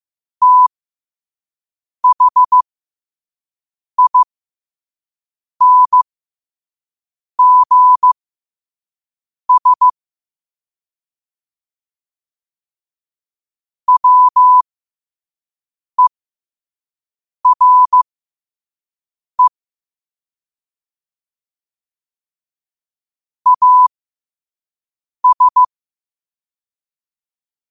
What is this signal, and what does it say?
Morse code, heard as a keyed tone, about 15 wpm, spaced as for 5 wpm, 1000 Hz. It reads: THINGS WERE AS
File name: qotdfiles / 5wpm